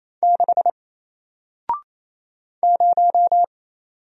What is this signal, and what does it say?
Morse code, heard as a keyed tone, about 27 wpm, 700 Hz. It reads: 6 0
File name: Single-Letter-Number-28wpm